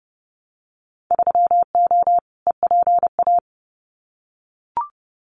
Morse code, heard as a keyed tone, about 30 wpm, 700 Hz. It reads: 3O EPA
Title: Winter Field Day Exchanges 30wpm